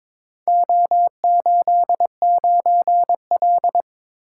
Morse code, heard as a keyed tone, about 22 wpm, 700 Hz. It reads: O89L
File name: practice / 22wpm / Four Letter-Number Combinations - 22wpm